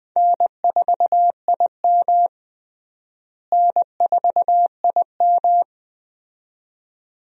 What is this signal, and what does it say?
Morse code, heard as a keyed tone, about 20 wpm, 700 Hz. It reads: N4IM N4IM
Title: USA Call Signs - 4 Characters - Repeated 2x - 20wpm